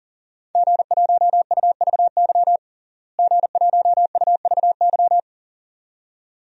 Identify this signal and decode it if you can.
Morse code, heard as a keyed tone, about 40 wpm, 700 Hz. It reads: G1UVY G1UVY
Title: DXCC-100 Call Signs - 5 Characters - Repeated 2x - 40wpm